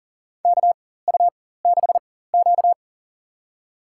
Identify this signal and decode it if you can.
Morse code, heard as a keyed tone, about 40 wpm, spaced as for 19 wpm, 700 Hz. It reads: KU6Q
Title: USA Call Signs - 4 Characters - 4x Character Spacing - 40wpm